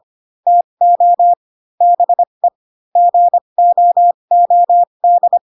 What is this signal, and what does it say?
Morse code, heard as a keyed tone, about 25 wpm, spaced as for 22 wpm, 700 Hz. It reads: TO BE GOOD
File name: Pinocchio / Pinocchio ch2 18 25 ik1hge